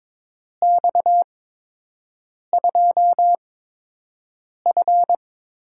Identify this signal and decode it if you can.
Morse code, heard as a keyed tone, about 22 wpm, spaced as for 6 wpm, 700 Hz. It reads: X2F
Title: USA Call Signs - 4 Characters - 8x Character Spacing - 22wpm